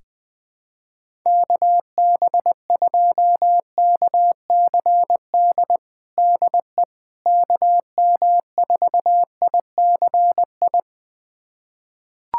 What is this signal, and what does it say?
Morse code, heard as a keyed tone, about 21 wpm, 700 Hz. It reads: KB2KCD DE KM4ICI
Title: USA Callsign DE Callsign 20wpm